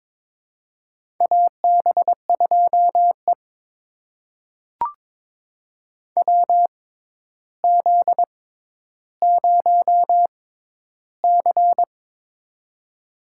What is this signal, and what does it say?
Morse code, heard as a keyed tone, about 22 wpm, spaced as for 11 wpm, 700 Hz. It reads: AB2E WZ0C
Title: USA Call Signs - 4 Characters - 6x Character Spacing - 22wpm